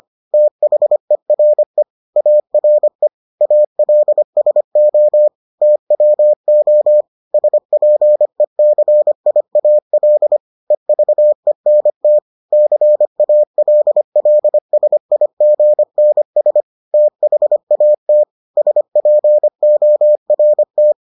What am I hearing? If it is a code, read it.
Morse code, heard as a keyed tone, about 25 wpm, 600 Hz. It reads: THERE ARE ALSO TWO SPECIAL EVENT CALLSIGNS THAT SPORT